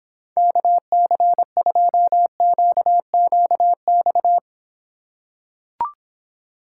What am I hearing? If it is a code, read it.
Morse code, heard as a keyed tone, about 26 wpm, 700 Hz. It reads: KC2QQX